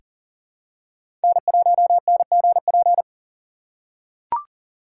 Morse code, heard as a keyed tone, about 40 wpm, 700 Hz. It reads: N1NGP
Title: USA Call Signs - 2x Character Spacing - 40wpm